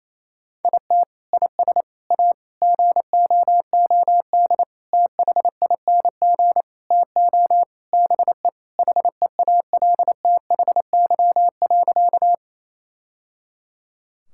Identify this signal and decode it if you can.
Morse code, heard as a keyed tone, about 28 wpm, 700 Hz. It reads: IT IS A GOOD THING TO BE HEALTHY.
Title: Sentences from Top 1000 Words - Review Part 5 - 28wpm